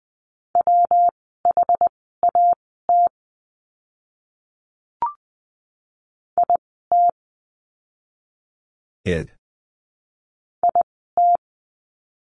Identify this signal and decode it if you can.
Morse code, heard as a keyed tone, about 20 wpm, spaced as for 15 wpm, 700 Hz. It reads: WHAT IT IT
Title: 199 Lesson 32f - Letter k - Words 20wpm